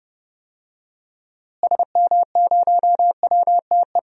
Morse code, heard as a keyed tone, about 30 wpm, 700 Hz. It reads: SM0WTE
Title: DXCC-100 Call Signs - 6 Characters - Repeated 2x - 30wpm